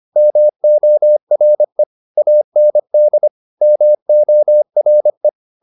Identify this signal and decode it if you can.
Morse code, heard as a keyed tone, about 25 wpm, 600 Hz. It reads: MORE AND MORE